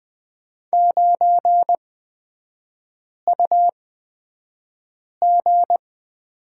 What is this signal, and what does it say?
Morse code, heard as a keyed tone, about 20 wpm, spaced as for 5 wpm, 700 Hz. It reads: 9UG
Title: 224 Lesson 36e - BT - Three Characters 20wpm